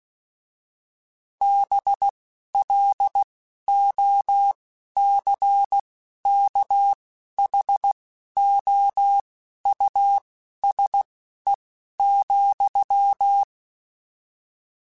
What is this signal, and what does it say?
Morse code, heard as a keyed tone, about 16 wpm, spaced as for 11 wpm, 800 Hz. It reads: BLOCKHOUSE,
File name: General / General-mp3s / Houses @10